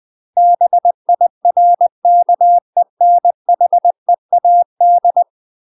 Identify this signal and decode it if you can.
Morse code, heard as a keyed tone, about 20 wpm, 700 Hz. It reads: BIRKENHEAD
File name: UK MASTER QSOs TRAINING / 20WPM / UK MASTER QSOs 0042